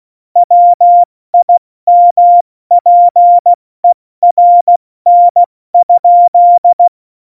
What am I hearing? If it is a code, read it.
Morse code, heard as a keyed tone, about 16 wpm, 700 Hz. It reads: WIMPERN?
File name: Morsekurs / Witze / Witz 06. November 2018, 70 F80